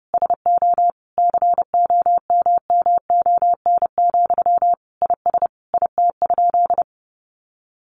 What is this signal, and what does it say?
Morse code, heard as a keyed tone, about 30 wpm, 700 Hz. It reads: SO COMMON, IS IT?
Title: Sentences from Top 700 Words 30wpm